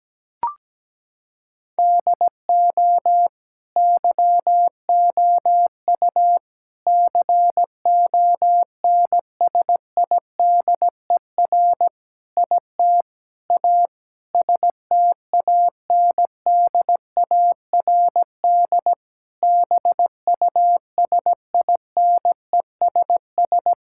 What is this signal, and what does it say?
Morse code, heard as a keyed tone, about 17 wpm, 700 Hz. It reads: DO YOU CONSIDER IT A STANDARD BUSINESS
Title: Sentences from Top 1000 Words - Review Part 6 - 17wpm